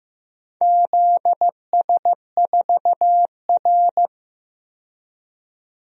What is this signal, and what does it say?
Morse code, heard as a keyed tone, about 15 wpm, 700 Hz. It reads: ZS4R
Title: DXCC-100 Call Signs - 4 Characters - 15wpm